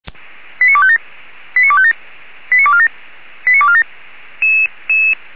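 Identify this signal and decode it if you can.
Five-tone selective call (ZVEI), heard as quick runs of five tones, tones of about 80 ms. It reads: 89257 89257 89247 89247
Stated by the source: received on SDR, ZVEI five-tone selective call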